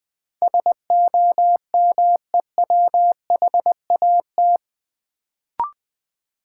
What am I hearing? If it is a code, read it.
Morse code, heard as a keyed tone, about 20 wpm, 700 Hz. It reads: SOMEWHAT
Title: Sentences from Top 1000 Words - Review Part 1 - New Word Drill - 20wpm